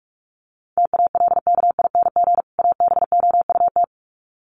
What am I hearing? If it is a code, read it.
Morse code, heard as a keyed tone, about 44 wpm, 700 Hz. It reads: TALKING ABOUT
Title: Sets of 2 Words 45wpm